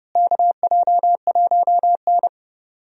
Morse code, heard as a keyed tone, about 30 wpm, 700 Hz. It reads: KJ1N
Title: USA Call Signs - 4 Characters - Repeated 3x - 30wpm